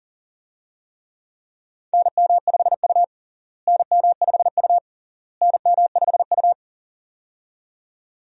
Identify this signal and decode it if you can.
Morse code, heard as a keyed tone, about 40 wpm, 700 Hz. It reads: NM5U NM5U NM5U
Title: USA Call Signs - 4 Characters - Repeated 3x - 40wpm